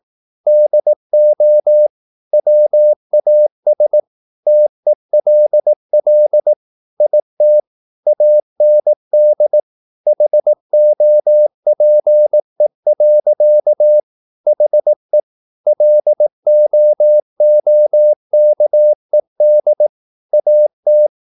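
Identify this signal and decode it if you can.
Morse code, heard as a keyed tone, about 18 wpm, 600 Hz. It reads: DO WAS TELL IT AND HOPE. HE LOOKED AT
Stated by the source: Morse practice at 18 wpm, English text, from CW med Gnister.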